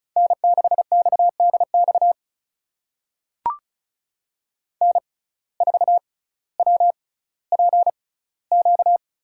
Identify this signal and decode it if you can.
Morse code, heard as a keyed tone, about 36 wpm, spaced as for 18 wpm, 700 Hz. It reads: N6XDX N4WPQ
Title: USA Call Signs - 5 Characters - 6x Character Spacing - 35wpm